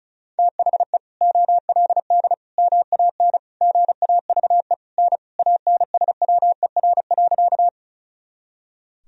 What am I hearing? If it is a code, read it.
Morse code, heard as a keyed tone, about 36 wpm, 700 Hz. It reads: THE OLD MAN GAVE N ANSWER.
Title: Sentences from Top 500 Words and Abbreviations 35wpm